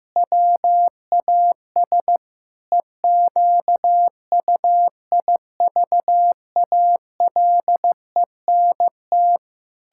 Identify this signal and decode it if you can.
Morse code, heard as a keyed tone, about 15 wpm, 700 Hz. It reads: WAS EQUIVALENT
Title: Sentences from Top 1600 Words 15wpm